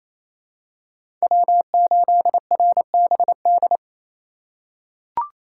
Morse code, heard as a keyed tone, about 28 wpm, 700 Hz. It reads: W8RBD
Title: USA Call Signs - 3x Character Spacing - 28wpm